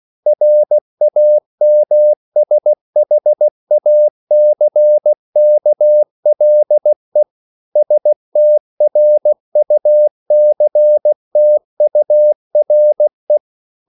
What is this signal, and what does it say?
Morse code, heard as a keyed tone, about 16 wpm, 600 Hz. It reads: RAMSHACKLE STRUCTURE